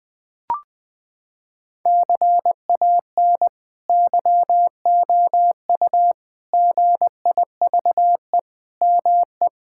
Morse code, heard as a keyed tone, about 19 wpm, 700 Hz. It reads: CAN YOU GIVE ME
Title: Sentences from Top 1000 Words - Review Part 6 - 20wpm